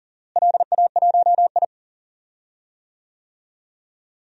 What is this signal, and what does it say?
Morse code, heard as a keyed tone, about 40 wpm, 700 Hz. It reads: LA1I
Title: DXCC-100 Call Signs - 4 Characters - Repeated 3x - 40wpm